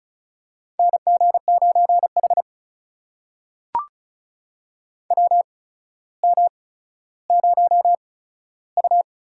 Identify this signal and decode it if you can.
Morse code, heard as a keyed tone, about 36 wpm, spaced as for 15 wpm, 700 Hz. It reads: NG9H WM0U
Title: USA Call Signs - 4 Characters - 8x Character Spacing - 35wpm